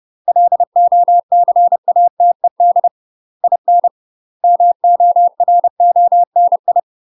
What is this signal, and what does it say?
Morse code, heard as a keyed tone, about 30 wpm, 700 Hz. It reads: LOCATED IN MORONI
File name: UK MASTER QSOs TRAINING / 30WPM / UK MASTER QSOs 0024